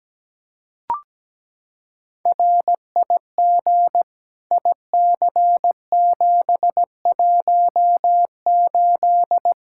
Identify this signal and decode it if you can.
Morse code, heard as a keyed tone, about 17 wpm, 700 Hz. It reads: RIG IC718
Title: K1USN SST QSO Practice 17wpm